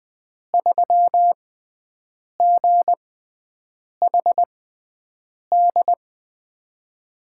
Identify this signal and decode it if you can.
Morse code, heard as a keyed tone, about 20 wpm, spaced as for 7 wpm, 700 Hz. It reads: 3GHD
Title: USA Call Signs - 5 Characters - 6x Character Spacing - 20wpm